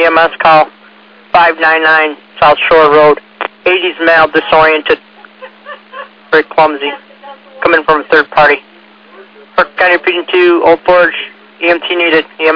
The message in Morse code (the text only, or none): none